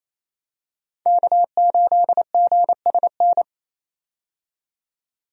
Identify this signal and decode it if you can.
Morse code, heard as a keyed tone, about 27 wpm, 700 Hz. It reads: K8GSN
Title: USA Call Signs - 5 Characters - 28wpm